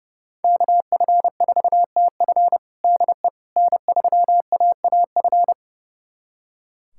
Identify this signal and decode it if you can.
Morse code, heard as a keyed tone, about 32 wpm, 700 Hz. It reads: KF4TF DE N3AAF
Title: USA Callsign DE Callsign - 5 Characters - 30wpm